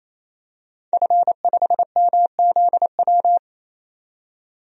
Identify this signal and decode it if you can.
Morse code, heard as a keyed tone, about 28 wpm, 700 Hz. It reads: F5MZW
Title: DXCC-100 Call Signs - 5 Characters - 28wpm